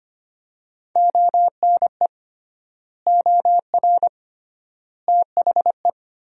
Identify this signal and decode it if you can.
Morse code, heard as a keyed tone, about 25 wpm, 700 Hz. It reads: ONE OR THE
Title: 3x Word-Spacing - Sentences from Top 400 Words - 25wpm